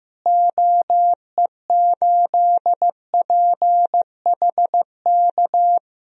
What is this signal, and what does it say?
Morse code, heard as a keyed tone, about 15 wpm, 700 Hz. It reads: OE8PHK
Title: DXCC-100 Call Signs - 6 Characters - Repeated 3x - 15wpm